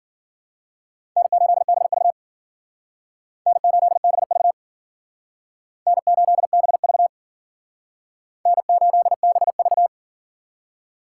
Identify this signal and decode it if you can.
Morse code, heard as a keyed tone, about 51 wpm, 700 Hz. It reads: N8BV N8BV N8BV N8BV